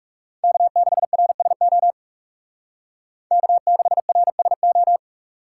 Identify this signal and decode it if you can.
Morse code, heard as a keyed tone, about 36 wpm, 700 Hz. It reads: K6RSO K6RSO
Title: Speed Racing - USA 1x3 Call Signs 30wpm